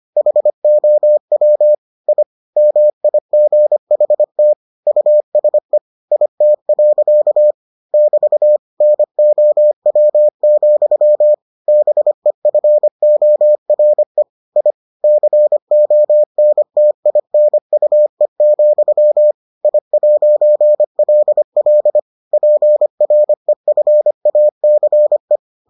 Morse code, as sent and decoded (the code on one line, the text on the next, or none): .... --- .-- / .. / -- .. --. .... - / ..- ... / .. - .-.-.- / -...- / -. --- .-- --..-- / -... ..-. --- .-. / .. / -.-. --- -. - .. -. ..- . --..-- / .. .----. .-.. .-.. / .--. .-. ..-. .- -.-. .
HOW I MIGHT USE IT. <BT> NOW, BEFORE I CONTINUE, I'LL PREFACE